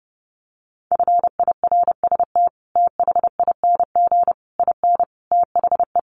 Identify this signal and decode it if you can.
Morse code, heard as a keyed tone, about 30 wpm, 700 Hz. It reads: FIRST THING IN THE
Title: Sets of 5 Words Encore2 30wpm